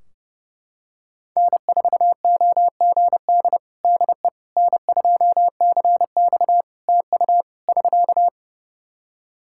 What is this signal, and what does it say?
Morse code, heard as a keyed tone, about 30 wpm, 700 Hz. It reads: N4OGD DE N2CX TU <SK>
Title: Practice QSO - Exchange 4 - Ending - 30wpm